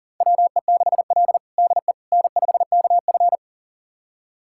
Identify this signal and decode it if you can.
Morse code, heard as a keyed tone, about 40 wpm, 700 Hz. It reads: WE6L DE N5KF
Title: USA Callsign DE Callsign - 4 Characters - 40wpm